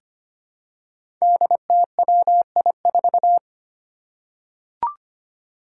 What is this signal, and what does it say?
Morse code, heard as a keyed tone, about 25 wpm, 700 Hz. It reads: DTWI4
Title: Five Challenging Letter-Number Combinations 25wpm